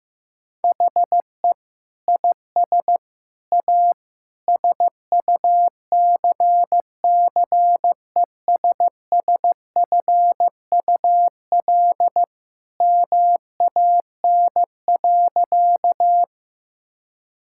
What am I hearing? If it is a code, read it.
Morse code, heard as a keyed tone, about 15 wpm, 700 Hz. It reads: HE IS A SUCCESSFUL MAN.
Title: Sentences from Top 1000 Words - Review Part 6 - 15wpm